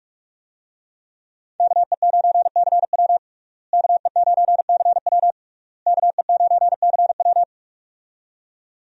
Morse code, heard as a keyed tone, about 45 wpm, 700 Hz. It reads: KE9CW KE9CW KE9CW
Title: USA Call Signs - 5 Characters - Repeated 3x - 45wpm